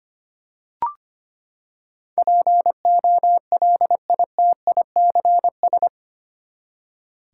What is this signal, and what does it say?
Morse code, heard as a keyed tone, about 25 wpm, 700 Hz. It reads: POLITICS
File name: Eight Letter Words - 25wpm